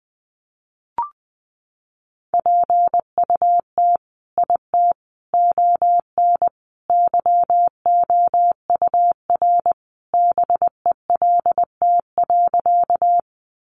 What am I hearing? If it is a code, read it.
Morse code, heard as a keyed tone, about 20 wpm, 700 Hz. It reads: PUT IT ON YOUR BELT.
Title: Sentences from Top 1400 Words 20wpm